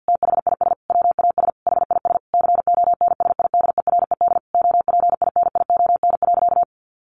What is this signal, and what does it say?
Morse code, heard as a keyed tone, about 50 wpm, 700 Hz. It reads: THIS WAS HIS CONSIDERED OPINION.
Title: A Sound Of Thunder 50wpm